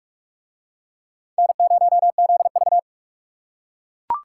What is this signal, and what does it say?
Morse code, heard as a keyed tone, about 44 wpm, 700 Hz. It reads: N0ZV